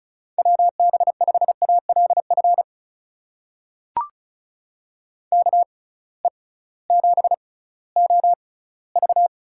Morse code, heard as a keyed tone, about 36 wpm, spaced as for 19 wpm, 690 Hz. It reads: WB5ALF KE7OV
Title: USA Call Signs - 6 Characters - 6x Character Spacing - 35wpm